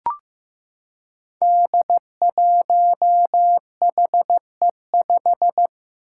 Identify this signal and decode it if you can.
Morse code, heard as a keyed tone, about 15 wpm, 700 Hz. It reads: D1HE5
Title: Five Challenging Letter-Number Combinations 15wpm